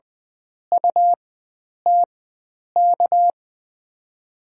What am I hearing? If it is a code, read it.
Morse code, heard as a keyed tone, about 21 wpm, spaced as for 9 wpm, 700 Hz. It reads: UTK
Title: USA Call Signs - 5 Characters - 4x Character Spacing - 20wpm